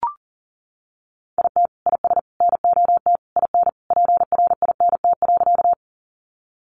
Morse code, heard as a keyed tone, about 40 wpm, 700 Hz. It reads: IT IS NOT IN PRINT.
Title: Sentences from Top 1400 Words 40wpm